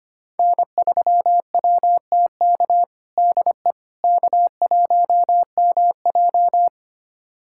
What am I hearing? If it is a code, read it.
Morse code, heard as a keyed tone, about 25 wpm, 700 Hz. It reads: N3WTK DE K1MJ
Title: USA Callsign DE Callsign 25wpm